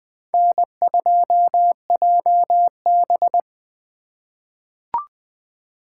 Morse code, heard as a keyed tone, about 20 wpm, 700 Hz. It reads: N2JB
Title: USA Call Signs - 2x Character Spacing - 20wpm